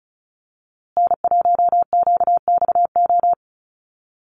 Morse code, heard as a keyed tone, about 34 wpm, 700 Hz. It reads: N1QXO
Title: USA Call Signs Repeated 2x 35wpm